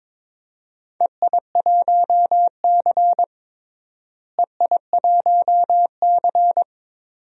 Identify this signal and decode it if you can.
Morse code, heard as a keyed tone, about 22 wpm, 700 Hz. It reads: EI1C EI1C
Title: DXCC-100 Call Signs - 4 Characters - Repeated 2x - 22wpm